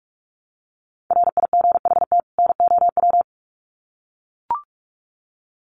Morse code, heard as a keyed tone, about 44 wpm, 700 Hz. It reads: RIGHT NOW
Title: Sets of 2 Words 45wpm